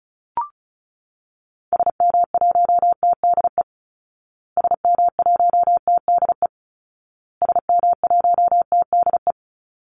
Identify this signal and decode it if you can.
Morse code, heard as a keyed tone, about 34 wpm, 700 Hz. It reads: SM1TDE SM1TDE SM1TDE
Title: Call Signs Repeated 2x - 35wpm